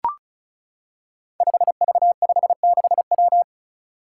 Morse code, heard as a keyed tone, about 36 wpm, 700 Hz. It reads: 5V56W
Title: Five Challenging Letter-Number Combinations 35wpm